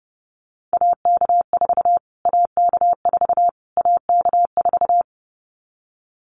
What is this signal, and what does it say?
Morse code, heard as a keyed tone, about 30 wpm, 700 Hz. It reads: AK4 AK4 AK4
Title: Callsign Prefixes for Top 25 DXCC - Repeated 3x - 30wpm